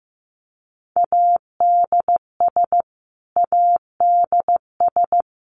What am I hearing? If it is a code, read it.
Morse code, heard as a keyed tone, about 15 wpm, 700 Hz. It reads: ADS ADS
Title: Abbreviations Repeated 3x - 15wpm